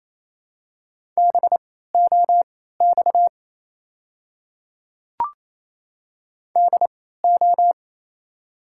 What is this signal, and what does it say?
Morse code, heard as a keyed tone, about 28 wpm, spaced as for 16 wpm, 700 Hz. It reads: BOX DO
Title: 3x Character Spacing - Top 500 Words - 28wpm